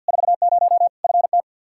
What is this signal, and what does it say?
Morse code, heard as a keyed tone, about 52 wpm, 700 Hz. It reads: 40 FT
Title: Practice QSO - Exchange 2 - Info - 50wpm